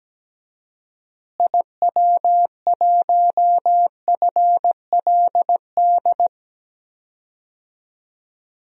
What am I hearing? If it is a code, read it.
Morse code, heard as a keyed tone, about 17 wpm, 700 Hz. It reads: IW1FLD
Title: DXCC-100 Call Signs - 6 Characters - Repeated 3x - 17wpm